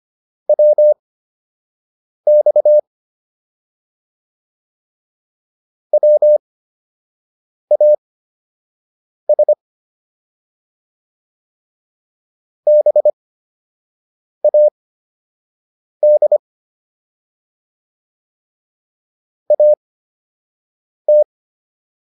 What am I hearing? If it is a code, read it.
Morse code, heard as a keyed tone, about 25 wpm, spaced as for 6 wpm, 600 Hz. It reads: WX WAS BAD AT